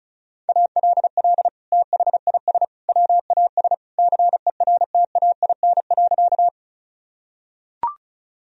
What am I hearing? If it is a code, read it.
Morse code, heard as a keyed tone, about 36 wpm, 700 Hz. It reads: ALL THIS WAS CERTAIN.
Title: Sentences from Top 1000 Words - Review Part 2 - 35wpm